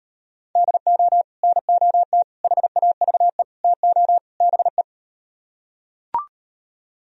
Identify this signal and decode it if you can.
Morse code, heard as a keyed tone, about 38 wpm, 700 Hz. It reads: DO NOT HAVE TO BE